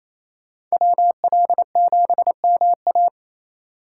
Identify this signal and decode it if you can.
Morse code, heard as a keyed tone, about 28 wpm, 700 Hz. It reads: WL7MA
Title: USA Call Signs - 5 Characters - 8x Character Spacing - 28wpm